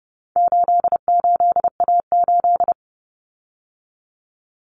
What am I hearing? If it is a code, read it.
Morse code, heard as a keyed tone, about 30 wpm, 700 Hz. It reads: 88A8
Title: Common Cut Numbers - 4-Digit 30wpm